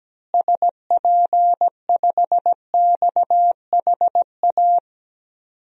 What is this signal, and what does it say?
Morse code, heard as a keyed tone, about 17 wpm, 700 Hz. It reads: SP5XHA
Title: DXCC-100 Call Signs - 6 Characters - Repeated 2x - 17wpm